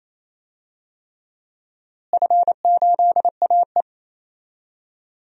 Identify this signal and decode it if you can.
Morse code, heard as a keyed tone, about 28 wpm, 700 Hz. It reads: F8AE
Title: DXCC-100 Call Signs - 4 Characters - Repeated 3x - 28wpm